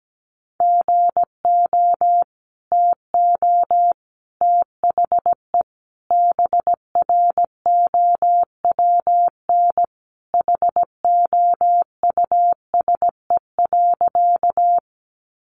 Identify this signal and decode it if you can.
Morse code, heard as a keyed tone, about 17 wpm, 700 Hz. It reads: GO TO THE BROWN HOUSE.
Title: Sentences from Top 1200 Words 17wpm